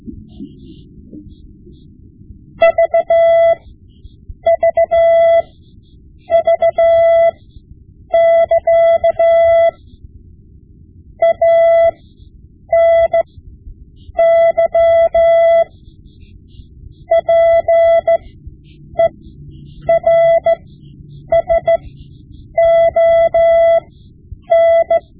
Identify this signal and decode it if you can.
Morse code, heard as a keyed tone, about 14 wpm, spaced as for 8 wpm, 670 Hz. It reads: VVV<KA> ANY PERSON
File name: Morse Lessons / Morse Lessons / Tape-5B